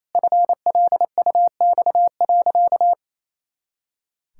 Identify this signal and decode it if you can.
Morse code, heard as a keyed tone, about 28 wpm, 700 Hz. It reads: FLUX.
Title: Pangram Challenge v2 - 28wpm